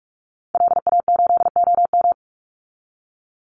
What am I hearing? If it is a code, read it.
Morse code, heard as a keyed tone, about 44 wpm, 700 Hz. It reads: LA8OM
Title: Call Signs - 45wpm